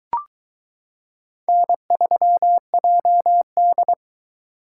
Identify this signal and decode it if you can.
Morse code, heard as a keyed tone, about 21 wpm, 700 Hz. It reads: N3JD